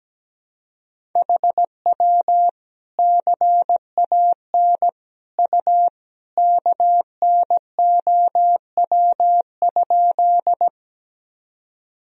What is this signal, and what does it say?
Morse code, heard as a keyed tone, about 17 wpm, 700 Hz. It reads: HW CAN U KNOW?
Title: Sentences from Top 100 Words and Abbreviations 17wpm